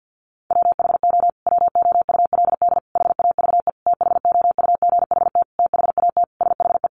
Morse code, heard as a keyed tone, about 50 wpm, 700 Hz. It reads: WHO WOULD HAVE THOUGHT THAT SHE
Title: Sentences from Top 1200 Words 50wpm